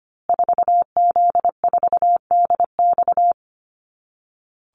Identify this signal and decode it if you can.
Morse code, heard as a keyed tone, about 25 wpm, 700 Hz. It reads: 4Z4DX
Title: Call Signs Repeated 2x - 25wpm